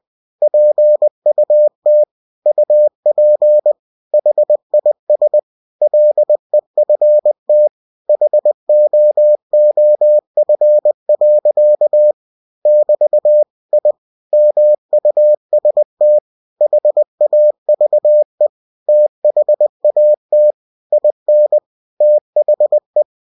Morse code, heard as a keyed tone, about 20 wpm, 600 Hz. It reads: PUT UP HIS LEFT HOOF. <BT> I MUST HAVE THAT IN THE
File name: cw-books / winnie-the-pooh / winnie-20wpm-page-0055